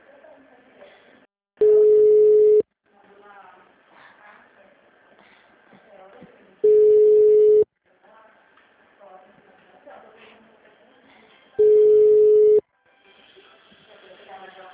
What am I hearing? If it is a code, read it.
Ringback tone (Europe, 425 Hz), with speech.
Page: Cosmopolis - SP